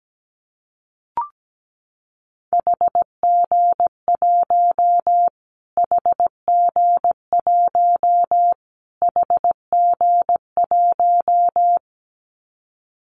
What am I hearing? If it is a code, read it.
Morse code, heard as a keyed tone, about 17 wpm, 700 Hz. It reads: HG1 HG1 HG1
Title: Callsign Prefixes for Top 25 DXCC - Repeated 3x - 17wpm